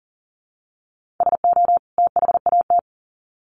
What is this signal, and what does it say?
Morse code, heard as a keyed tone, about 40 wpm, 700 Hz. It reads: SO THAT
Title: Sets of 2 Words 40wpm